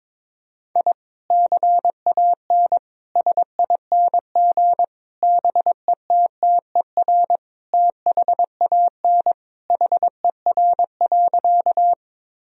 Morse code, heard as a keyed tone, about 22 wpm, 700 Hz. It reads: I CAN SING BETTER THAN HER.
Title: Sentences from Top 1000 Words - Review Part 2 - 22wpm